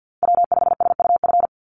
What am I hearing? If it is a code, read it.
Morse code, heard as a keyed tone, about 49 wpm, 700 Hz. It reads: W5SUF
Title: USA Call Signs 50wpm